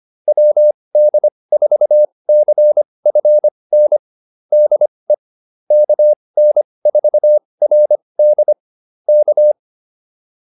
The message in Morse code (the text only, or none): WD4CFN DE KN4RD K